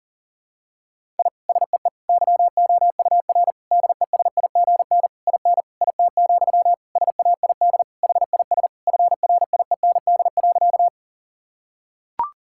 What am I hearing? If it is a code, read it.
Morse code, heard as a keyed tone, about 40 wpm, 700 Hz. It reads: I SEE YOUR DESIGN IN IT, SAID HIS FRIEND.
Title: Sentences from Top 1000 Words - Review Part 5 - 40wpm